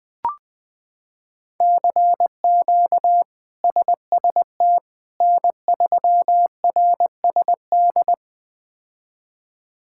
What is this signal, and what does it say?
Morse code, heard as a keyed tone, about 20 wpm, 700 Hz. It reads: CQ SST N3RSD
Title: K1USN SST QSO Practice - 20wpm